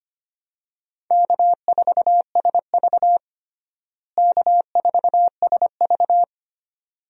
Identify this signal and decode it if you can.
Morse code, heard as a keyed tone, about 25 wpm, 700 Hz. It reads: K4SV K4SV
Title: USA Call Signs - 4 Characters - Repeated 2x - 25wpm